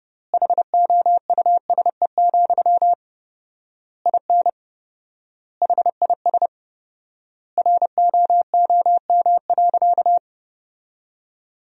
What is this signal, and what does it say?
Morse code, heard as a keyed tone, about 30 wpm, 700 Hz. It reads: HOUSE, IN HIS ROOM.
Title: 4x Word-Spacing - Sentences from Top 300 Words - 30wpm